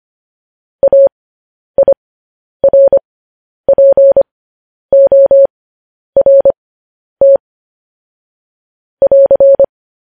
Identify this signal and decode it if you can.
Morse code, heard as a keyed tone, about 25 wpm, spaced as for 10 wpm, 550 Hz. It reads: AIRPORT <AR>
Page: news headlines in morse code at 10 wpm